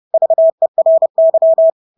Morse code, heard as a keyed tone, about 30 wpm, 650 Hz. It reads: VERY
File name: Engelske ord 30-30 wpm